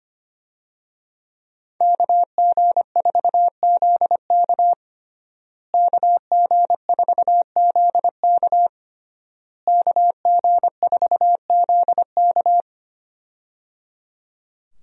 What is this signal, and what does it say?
Morse code, heard as a keyed tone, about 25 wpm, 700 Hz. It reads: KG4ZK KG4ZK KG4ZK
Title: USA Call Signs - 5 Characters - Repeated 3x - 25wpm